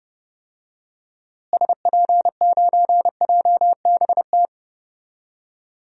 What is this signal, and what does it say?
Morse code, heard as a keyed tone, about 30 wpm, 700 Hz. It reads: SP9JBT
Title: DXCC-100 Call Signs - 6 Characters - Repeated 3x - 30wpm